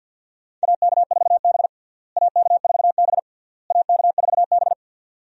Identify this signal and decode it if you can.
Morse code, heard as a keyed tone, about 52 wpm, 700 Hz. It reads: AK4B AK4B AK4B
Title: USA Call Signs - 4 Characters - Repeated 3x - 50wpm